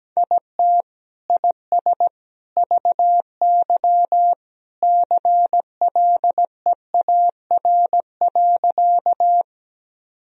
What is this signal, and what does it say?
Morse code, heard as a keyed tone, about 17 wpm, 700 Hz. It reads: IT IS VY CLEAR.
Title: Sentences from Top 500 Words and Abbreviations 17wpm